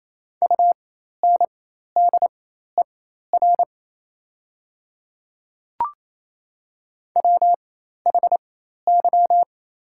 Morse code, heard as a keyed tone, about 28 wpm, spaced as for 13 wpm, 690 Hz. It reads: UNDER WHY
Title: 4x Character Spacing - Top 200 Words - 28wpm